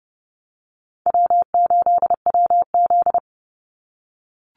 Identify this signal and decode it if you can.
Morse code, heard as a keyed tone, about 30 wpm, 700 Hz. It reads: W8WZ
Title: Call Signs - 30wpm